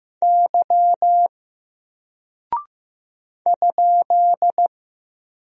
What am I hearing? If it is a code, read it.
Morse code, heard as a keyed tone, about 15 wpm, 700 Hz. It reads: Y ?
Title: Single Character (Letter:Number:Punctuation:Prosign) Alt 15wpm